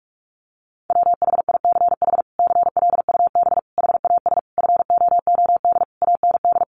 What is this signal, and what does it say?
Morse code, heard as a keyed tone, about 44 wpm, 700 Hz. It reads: WHICH CLUB HAS FOOD AND
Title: Sentences from Top 1200 Words 45wpm